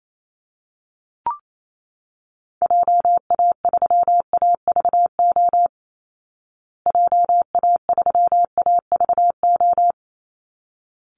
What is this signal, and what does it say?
Morse code, heard as a keyed tone, about 28 wpm, 700 Hz. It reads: JA3AVO JA3AVO
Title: Call Signs Repeated 3x - 28wpm